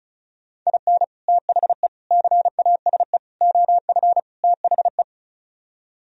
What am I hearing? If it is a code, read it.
Morse code, heard as a keyed tone, about 42 wpm, 700 Hz. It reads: IN THE CASE OF THE
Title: Speed-Racing - Sets of 5 Words 35wpm